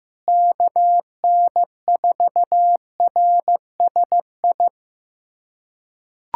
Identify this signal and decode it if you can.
Morse code, heard as a keyed tone, about 15 wpm, 700 Hz. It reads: KN4RSI
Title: USA Callsign DE Callsign - 6 Characters - 15wpm